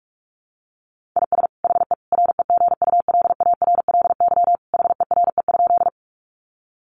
Morse code, heard as a keyed tone, about 44 wpm, 700 Hz. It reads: IS HE REGULARLY HERE?
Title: Sentences from Top 1400 Words 45wpm